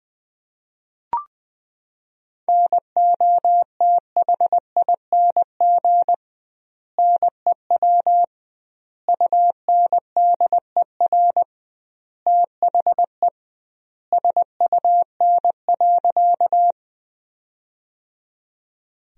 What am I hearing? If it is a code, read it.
Morse code, heard as a keyed tone, about 20 wpm, 700 Hz. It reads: NOTHING NEW UNDER THE SUN.
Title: 2x Word-Spacing - Sentences from Top 500 Words - 20wpm